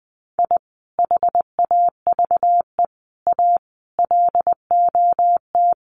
Morse code, heard as a keyed tone, about 20 wpm, 700 Hz. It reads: I HAVE A LOT
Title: Sets of 5 Words 20wpm